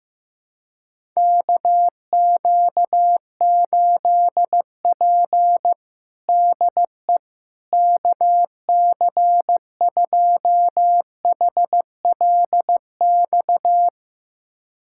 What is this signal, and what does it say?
Morse code, heard as a keyed tone, about 15 wpm, 700 Hz. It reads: KQ8P DE KC2HLX
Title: USA Callsign DE Callsign 15wpm